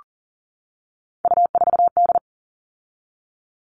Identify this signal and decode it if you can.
Morse code, heard as a keyed tone, about 40 wpm, 700 Hz. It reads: U4D
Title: All Cut Numbers - 3-Digit 40wpm